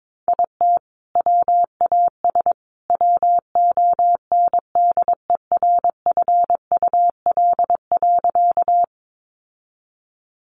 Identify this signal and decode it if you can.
Morse code, heard as a keyed tone, about 22 wpm, 700 Hz. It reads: IT WAS WONDERFUL.
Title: Sentences from Top 1100 Words 22wpm